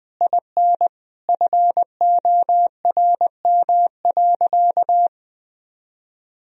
Morse code, heard as a keyed tone, about 20 wpm, 700 Hz. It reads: IN FORM.
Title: Sentences from Top 1000 Words - Review Part 1 - 20wpm